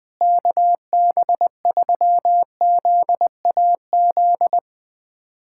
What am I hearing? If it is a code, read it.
Morse code, heard as a keyed tone, about 20 wpm, 700 Hz. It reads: KB3ZAZ
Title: USA Callsign DE Callsign - 6 Characters - 20wpm